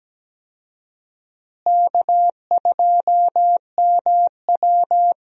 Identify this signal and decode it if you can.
Morse code, heard as a keyed tone, about 17 wpm, 700 Hz. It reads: K2MW
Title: USA Call Signs - 4 Characters - Repeated 2x - 17wpm